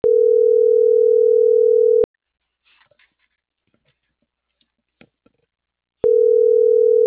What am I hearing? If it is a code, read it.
Ringback tone (North America), 440 + 480 Hz.